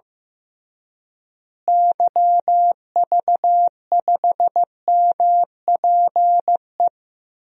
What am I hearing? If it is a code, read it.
Morse code, heard as a keyed tone, about 15 wpm, 700 Hz. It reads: YV5MPE
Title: DXCC-100 Call Signs - 6 Characters - Repeated 3x - 15wpm